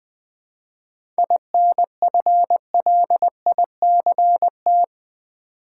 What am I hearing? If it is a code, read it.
Morse code, heard as a keyed tone, about 20 wpm, 690 Hz. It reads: INFLICT
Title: 7 Letter Words - First 3 Letters Given - 20wpm